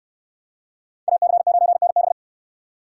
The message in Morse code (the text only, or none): AC9NB